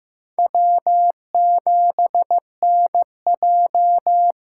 Morse code, heard as a keyed tone, about 15 wpm, 700 Hz. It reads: W7NJ
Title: USA Call Signs - 6x Character Spacing - 15wpm